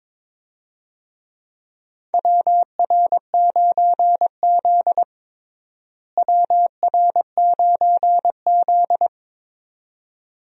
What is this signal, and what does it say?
Morse code, heard as a keyed tone, about 22 wpm, 700 Hz. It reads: WR9Z WR9Z
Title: USA Call Signs - 4 Characters - Repeated 2x - 22wpm